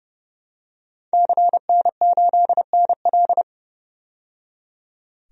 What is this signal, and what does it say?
Morse code, heard as a keyed tone, about 31 wpm, 700 Hz. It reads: CN8NL
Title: DXCC-100 Call Signs - 5 Characters - 30wpm